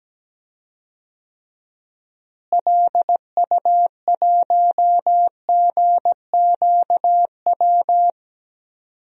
Morse code, heard as a keyed tone, about 17 wpm, 700 Hz. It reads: LU1GQW
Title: DXCC-100 Call Signs - 6 Characters - Repeated 2x - 17wpm